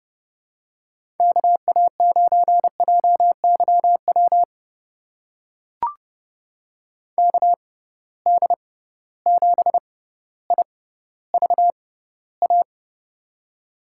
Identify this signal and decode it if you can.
Morse code, heard as a keyed tone, about 30 wpm, spaced as for 15 wpm, 700 Hz. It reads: KA9JYW KD7IVA
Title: USA Call Signs - 6x Character Spacing - 30wpm